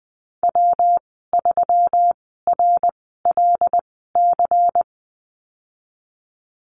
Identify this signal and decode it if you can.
Morse code, heard as a keyed tone, about 20 wpm, spaced as for 14 wpm, 700 Hz. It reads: W3RLC
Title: 147 Lesson 24g - Letter y - Call Signs 20wpm